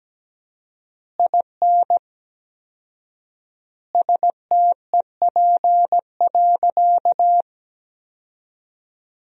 Morse code, heard as a keyed tone, about 17 wpm, 700 Hz. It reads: IN STEP.
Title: 4x Word-Spacing - Sentences from Top 400 Words - 17wpm